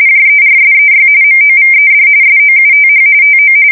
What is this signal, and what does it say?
RTTY, about 45.45 baud. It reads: JERSEY MOVED SOUTH TO
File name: fsk170 / battleshipnj